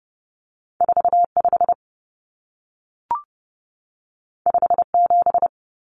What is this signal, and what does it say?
Morse code, heard as a keyed tone, about 30 wpm, 700 Hz. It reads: 45 57
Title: Numbers - 2-Digit 30wpm